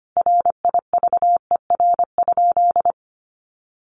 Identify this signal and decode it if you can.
Morse code, heard as a keyed tone, about 25 wpm, 700 Hz. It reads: RIVER?
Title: Sentences from Top 1400 Words 25wpm